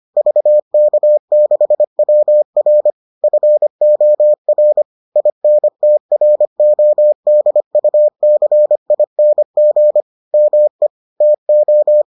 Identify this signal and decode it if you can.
Morse code, heard as a keyed tone, about 25 wpm, 600 Hz. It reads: VK6WR FOR INTRODUCING ME TO